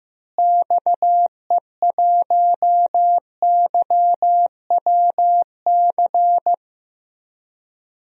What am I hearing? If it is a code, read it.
Morse code, heard as a keyed tone, about 16 wpm, 700 Hz. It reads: XE1YWC
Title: DXCC-100 Call Signs - 6 Characters - Repeated 3x - 15wpm